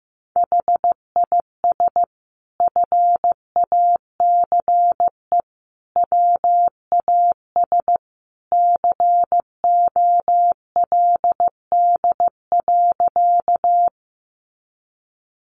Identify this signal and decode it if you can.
Morse code, heard as a keyed tone, about 15 wpm, 700 Hz. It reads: HIS FACE WAS COLD.
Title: A Sound Of Thunder 15wpm